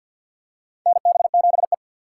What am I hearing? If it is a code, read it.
Morse code, heard as a keyed tone, about 48 wpm, 700 Hz. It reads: NB7E
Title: USA Call Signs - 4 Characters - 2x Character Spacing - 50wpm